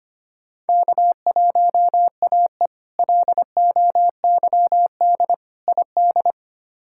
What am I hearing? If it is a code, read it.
Morse code, heard as a keyed tone, about 25 wpm, 700 Hz. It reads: K1AE LOYD ID
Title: K1USN SST QSO Practice - 25wpm